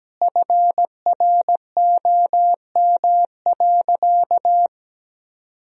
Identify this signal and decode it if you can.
Morse code, heard as a keyed tone, about 17 wpm, 700 Hz. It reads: FROM.
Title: Sentences from Top 1000 Words - Review Part 4 - 17wpm